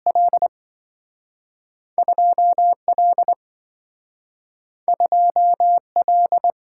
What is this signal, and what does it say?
Morse code, heard as a keyed tone, about 27 wpm, 700 Hz. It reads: L 2L 2L